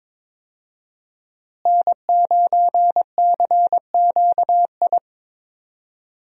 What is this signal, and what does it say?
Morse code, heard as a keyed tone, about 22 wpm, 700 Hz. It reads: N9CQI
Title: USA Call Signs - 5 Characters - Repeated 2x - 22wpm